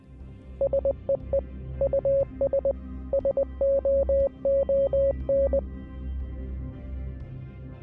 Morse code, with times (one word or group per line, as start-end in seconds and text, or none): 0.6-1.4 SEE
1.8-2.7 US
3.1-5.6 SOON